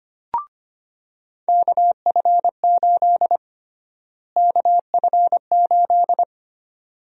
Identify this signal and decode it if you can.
Morse code, heard as a keyed tone, about 25 wpm, 700 Hz. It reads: KF8 KF8
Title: USA Call Sign Prefixes - Repeated 2x - 25wpm